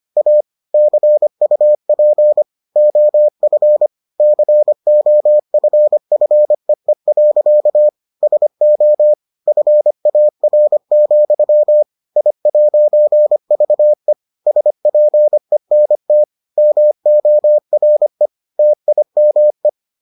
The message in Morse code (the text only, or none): A CUP OF COFFEE. SO FAR, I'VE SPENT MORE TIME